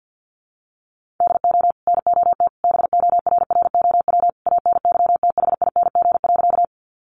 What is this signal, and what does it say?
Morse code, heard as a keyed tone, about 50 wpm, 700 Hz. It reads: DO NOT BORROW ANYTHING.
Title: Sentences from Top 1500 Words 50wpm